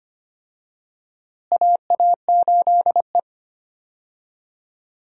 Morse code, heard as a keyed tone, about 25 wpm, 700 Hz. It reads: AA8E
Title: USA Call Signs - 4 Characters - Repeated 2x - 25wpm